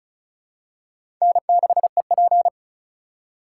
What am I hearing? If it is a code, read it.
Morse code, heard as a keyed tone, about 36 wpm, 700 Hz. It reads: N6EP
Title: USA Call Signs - 3x Character Spacing - 35wpm